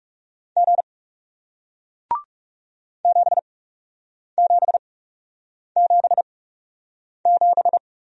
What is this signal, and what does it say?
Morse code, heard as a keyed tone, about 38 wpm, spaced as for 8 wpm, 700 Hz. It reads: G 7777